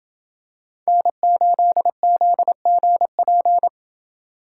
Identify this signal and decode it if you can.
Morse code, heard as a keyed tone, about 27 wpm, 700 Hz. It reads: N8ZGP